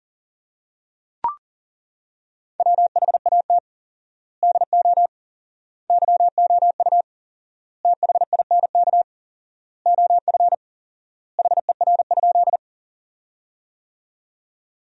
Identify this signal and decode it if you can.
Morse code, heard as a keyed tone, about 40 wpm, 700 Hz. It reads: WHAT DO YOU THINK OF HER?
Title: 4x Word-Spacing - Sentences from Top 200 Words - 40wpm